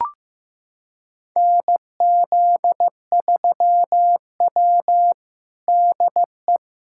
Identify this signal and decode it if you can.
Morse code, heard as a keyed tone, about 15 wpm, 700 Hz. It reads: NZ3W DE
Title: USA Callsign DE Callsign - 4 Characters - 15wpm